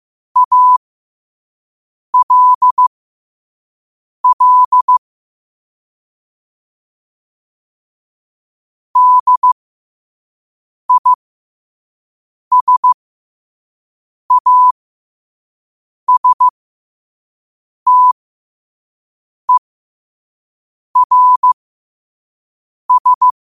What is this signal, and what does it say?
Morse code, heard as a keyed tone, about 15 wpm, spaced as for 5 wpm, 1000 Hz. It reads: ALL DISASTERS
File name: qotdfiles / 5wpm